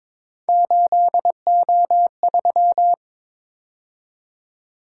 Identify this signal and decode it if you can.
Morse code, heard as a keyed tone, about 26 wpm, 700 Hz. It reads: 8O3
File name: Speed-Racing - Three Letter-Number Combinations - 22wpm